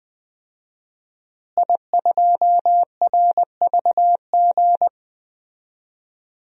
Morse code, heard as a keyed tone, about 20 wpm, 700 Hz. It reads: I2RVG
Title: DXCC-100 Call Signs - 5 Characters - Repeated 3x - 20wpm